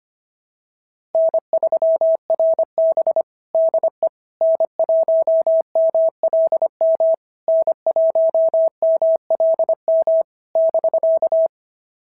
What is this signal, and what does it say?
Morse code, heard as a keyed tone, about 23 wpm, 650 Hz. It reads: N3RB DE N1MLM N1MLM <BK>
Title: SOTA QSO Practice 25wpm